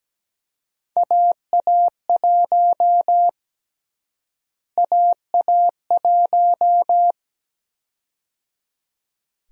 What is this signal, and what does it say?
Morse code, heard as a keyed tone, about 17 wpm, 700 Hz. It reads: AA1 AA1
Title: USA Call Sign Prefixes - Repeated 2x - 17wpm